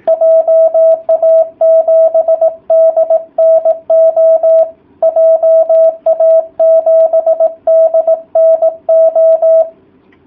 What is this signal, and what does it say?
Morse code, heard as a keyed tone, about 18 wpm, 660 Hz. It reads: JA7DNO JA7DNO